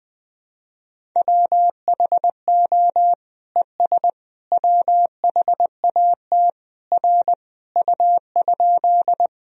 Morse code, heard as a keyed tone, about 20 wpm, 700 Hz. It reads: WHO ES WHAT R U?
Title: Sentences from Top 100 Words and Abbreviations 20wpm